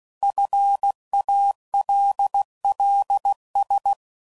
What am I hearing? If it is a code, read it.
Morse code, heard as a keyed tone, about 16 wpm, 800 Hz. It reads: FALLS
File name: CW Code Groups @15WPM / fromSM's / Cities